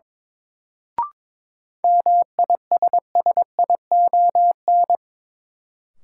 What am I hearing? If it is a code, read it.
Morse code, heard as a keyed tone, about 22 wpm, 700 Hz. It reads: MISSION
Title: Sentences from Top 1600 Words - New Word Drill Alt - 22wpm